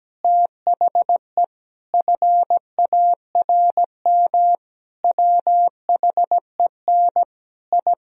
Morse code, heard as a keyed tone, about 17 wpm, 700 Hz. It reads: THE FARM WHEN I
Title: Sentences from Top 1000 Words - Review Part 2 - 17wpm